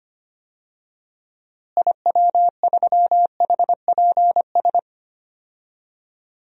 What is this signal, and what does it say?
Morse code, heard as a keyed tone, about 25 wpm, 700 Hz. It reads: IW3HPS
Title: DXCC-100 Call Signs - 6 Characters - Repeated 2x - 25wpm